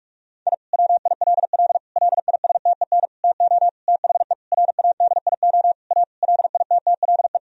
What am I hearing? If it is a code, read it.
Morse code, heard as a keyed tone, about 42 wpm, 700 Hz. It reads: I WILL LISTEN TO THE RADIO A LITTLE
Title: Sentences from Top 1000 Words - Review Part 3 - 45wpm